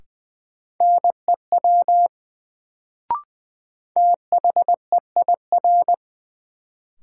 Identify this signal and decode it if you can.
Morse code, heard as a keyed tone, about 20 wpm, 700 Hz. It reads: NEW THEIR
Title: Sentences from Top 300 Words - New Word Drill - 20wpm